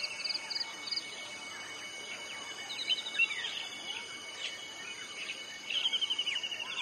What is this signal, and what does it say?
Dial-up modem handshake.
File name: wrct-log hi- 08